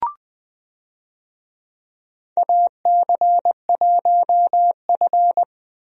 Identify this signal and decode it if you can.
Morse code, heard as a keyed tone, about 20 wpm, 700 Hz. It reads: AC1F
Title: USA Call Signs - 4 Characters - Repeated 2x - 20wpm